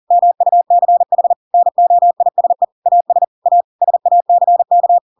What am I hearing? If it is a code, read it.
Morse code, heard as a keyed tone, about 40 wpm, 700 Hz. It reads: MUCH NOISE AS A SACK
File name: Pinocchio / Pinocchio ch7 40 40 ik1hge